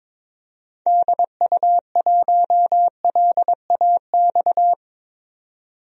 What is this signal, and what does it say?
Morse code, heard as a keyed tone, about 22 wpm, 700 Hz. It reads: DU1LAX
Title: DXCC-100 Call Signs - 6 Characters - 22wpm